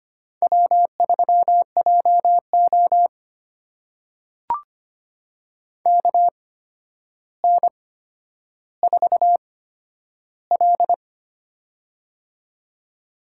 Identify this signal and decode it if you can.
Morse code, heard as a keyed tone, about 25 wpm, spaced as for 11 wpm, 700 Hz. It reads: W3JO KN4L
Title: USA Call Signs - 4 Characters - 8x Character Spacing - 25wpm